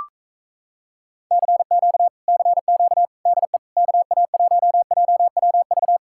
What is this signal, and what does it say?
Morse code, heard as a keyed tone, about 37 wpm, 700 Hz. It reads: CQ CQ DE KA1JWV